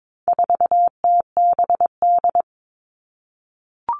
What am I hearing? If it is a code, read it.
Morse code, heard as a keyed tone, about 22 wpm, 700 Hz. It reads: 4TBD
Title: All Cut Numbers - 4-Digit 22wpm